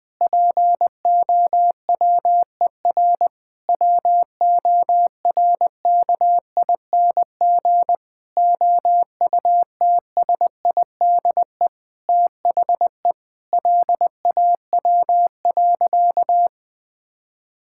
Morse code, heard as a keyed tone, about 21 wpm, 700 Hz. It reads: POWER WORKING OUTSIDE THE LAW.
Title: Sentences from Top 1000 Words - Review Part 6 - 20wpm